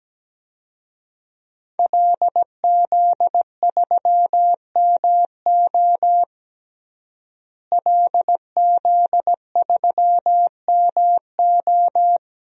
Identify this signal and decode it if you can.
Morse code, heard as a keyed tone, about 17 wpm, 700 Hz. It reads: LZ3MO LZ3MO
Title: DXCC-100 Call Signs - 5 Characters - Repeated 2x - 17wpm